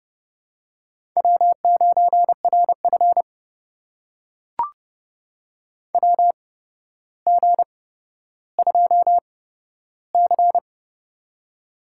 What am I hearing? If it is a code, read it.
Morse code, heard as a keyed tone, about 30 wpm, spaced as for 13 wpm, 700 Hz. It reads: W9RF WG2C
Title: USA Call Signs - 4 Characters - 8x Character Spacing - 30wpm